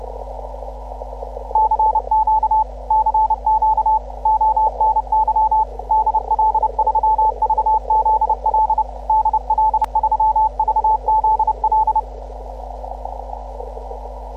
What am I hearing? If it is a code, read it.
Morse code, heard as a keyed tone, about 30 wpm, 870 Hz. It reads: CQ CQ ONY DL3VZL DL3VZL